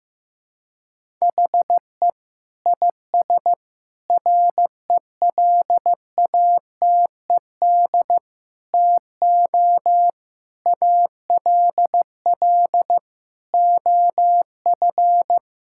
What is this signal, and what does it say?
Morse code, heard as a keyed tone, about 15 wpm, 700 Hz. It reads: HE IS RELATED TO ALL OF
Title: Sentences from Top 1000 Words - Review Part 5 - 15wpm